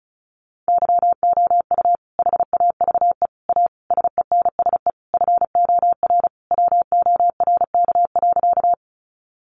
Morse code, heard as a keyed tone, about 34 wpm, 700 Hz. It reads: YOU HAVE A SENSE FOR WORK.
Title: Sentences from Top 700 Words 35wpm